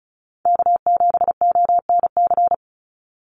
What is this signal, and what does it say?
Morse code, heard as a keyed tone, about 34 wpm, 700 Hz. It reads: K7ONC
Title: USA Call Signs 35wpm